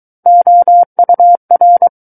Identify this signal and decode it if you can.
Morse code, heard as a keyed tone, about 23 wpm, 700 Hz. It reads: OUR
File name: Book One / 14-Chapter 14 - 23 WPM